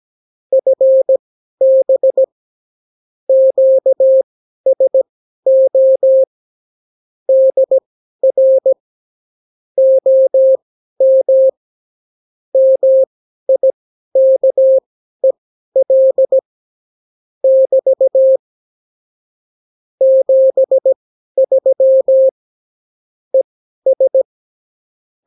Morse code, heard as a keyed tone, about 17 wpm, spaced as for 12 wpm, 530 Hz. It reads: FB QSO DR OM MIKEL <BT> 73 ES
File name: CW QSO libroCW4 18-8ppm